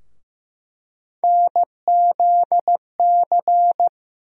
Morse code, heard as a keyed tone, about 15 wpm, 700 Hz. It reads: NZC
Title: USA Call Sign Postfixes - 3 Letters 15wpm